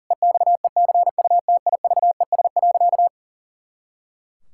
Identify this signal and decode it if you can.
Morse code, heard as a keyed tone, about 40 wpm, 700 Hz. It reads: EXECUTIVES.
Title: Pangram Challenge v2 - 40wpm